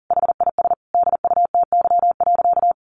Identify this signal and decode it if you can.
Morse code, heard as a keyed tone, about 40 wpm, 700 Hz. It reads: HIS DUTY.
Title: Sentences from Top 1500 Words 40wpm